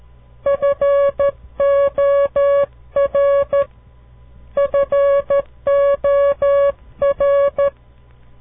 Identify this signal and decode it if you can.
Morse code, heard as a keyed tone, about 13 wpm, 560 Hz. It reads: FOR FOR
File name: CW Learning Programs / farnsworth / unit 2